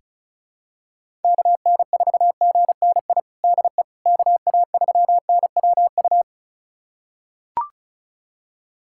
Morse code, heard as a keyed tone, about 34 wpm, 690 Hz. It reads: KN4GNI DE KA3NWU
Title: USA Callsign DE Callsign - 6 Characters - 35wpm